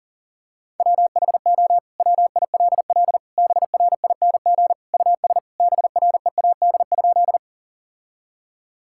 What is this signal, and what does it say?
Morse code, heard as a keyed tone, about 40 wpm, 700 Hz. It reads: WHO WILL BRING US BREAD?
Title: Sentences from Top 1600 Words 40wpm